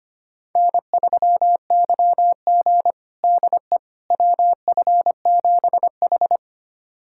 Morse code, heard as a keyed tone, about 25 wpm, 700 Hz. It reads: N3YG DE WF7H
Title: USA Callsign DE Callsign - 4 Characters - 25wpm